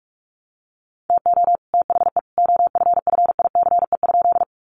Morse code, heard as a keyed tone, about 44 wpm, 700 Hz. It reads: TO THE OFFICE?
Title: Sentences from Top 1000 Words 45wpm